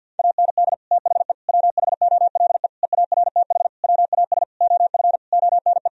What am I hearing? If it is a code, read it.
Morse code, heard as a keyed tone, about 52 wpm, 700 Hz. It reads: AND THE WHOLE EARTH WAS OF ONE
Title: Sentences from Top 1000 Words - Review Part 3 - 50wpm